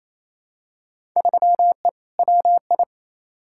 Morse code, heard as a keyed tone, about 28 wpm, 700 Hz. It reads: 3E WI
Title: Field Day Exchanges Encore - 28wpm